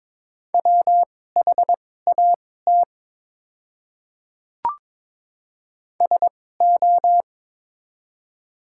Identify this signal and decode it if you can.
Morse code, heard as a keyed tone, about 23 wpm, spaced as for 16 wpm, 700 Hz. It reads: WHAT SO
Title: 2x Character Spacing - Top 300 Words - 22wpm